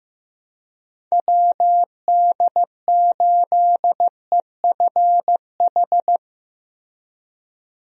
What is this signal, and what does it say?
Morse code, heard as a keyed tone, about 15 wpm, 700 Hz. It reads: WD8EFH
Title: USA Call Signs - 6 Characters - Repeated 3x - 15wpm